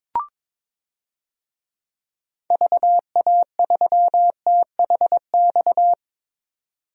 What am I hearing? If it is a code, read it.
Morse code, heard as a keyed tone, about 21 wpm, 700 Hz. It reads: VA3THX
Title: DXCC-100 Call Signs - 6 Characters - 22wpm